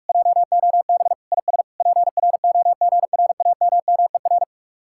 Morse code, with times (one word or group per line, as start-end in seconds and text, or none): 0.1-1.1 JOB
1.3-1.6 IS
1.8-4.4 PROGRAMMER